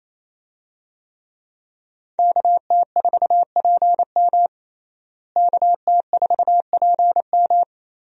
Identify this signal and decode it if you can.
Morse code, heard as a keyed tone, about 28 wpm, 700 Hz. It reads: KT4PM KT4PM
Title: USA Call Signs - 5 Characters - Repeated 2x - 28wpm